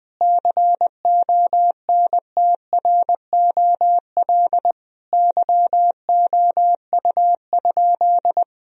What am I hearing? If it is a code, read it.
Morse code, heard as a keyed tone, about 20 wpm, 700 Hz. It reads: CONTROL YOU?
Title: Sentences from Top 1000 Words - Review Part 3 - 20wpm